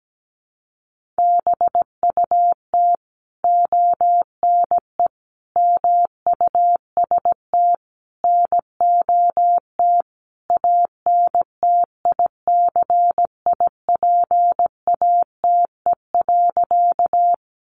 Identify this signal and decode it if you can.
Morse code, heard as a keyed tone, about 17 wpm, 700 Hz. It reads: BUT ONE MUST NOT ANTICIPATE.
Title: Sentences from Top 1500 Words 17wpm